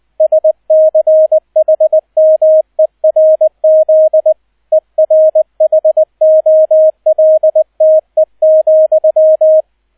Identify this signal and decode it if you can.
Morse code, heard as a keyed tone, about 20 wpm, 630 Hz, with radio noise. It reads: SCHMERZ ERHOLTE,
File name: 002 Zweiter Streich 20wpm